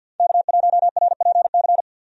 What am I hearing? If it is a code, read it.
Morse code, heard as a keyed tone, about 48 wpm, 700 Hz. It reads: K1RPC
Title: USA Call Signs - 5 Characters - 50wpm